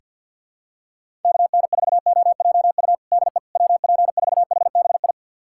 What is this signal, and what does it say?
Morse code, heard as a keyed tone, about 48 wpm, 700 Hz. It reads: KN4OJU DE WP4HBI
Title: USA Callsign DE Callsign 50wpm